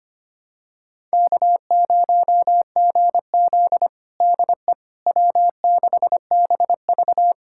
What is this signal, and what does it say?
Morse code, heard as a keyed tone, about 25 wpm, 700 Hz. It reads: K0GZ DE W6BV
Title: USA Callsign DE Callsign - 4 Characters - 25wpm